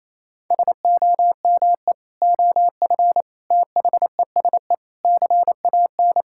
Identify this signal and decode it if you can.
Morse code, heard as a keyed tone, about 28 wpm, 700 Hz. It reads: SOME OF THESE CAN
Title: Sentences from Top 1000 Words - Review Part 6 - 28wpm